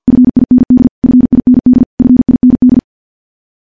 Morse code, heard as a keyed tone, about 46 wpm, 250 Hz. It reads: FIND FIND FIND
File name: morse-training-50WPM